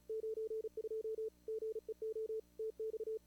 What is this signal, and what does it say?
Morse code, heard as a keyed tone, about 29 wpm, 430 Hz. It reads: 92 GEO TX
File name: CWSprint / Feb2022 / k5tr-feb-na-sprint-cw-k5ot -0000